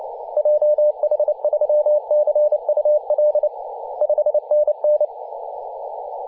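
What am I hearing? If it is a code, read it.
Morse code, heard as a keyed tone, about 30 wpm, 600 Hz, with radio noise. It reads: JH3CUL 5NN
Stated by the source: DX station in Papua New Guinea on 21.040 MHz in CW.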